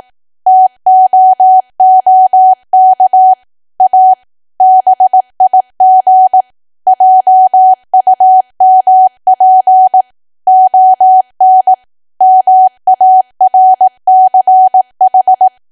Morse code, heard as a keyed tone, about 18 wpm, 750 Hz. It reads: TOOK A BIG JUMP ON MARCH
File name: cwa / W1AW-Level2 / 18WPM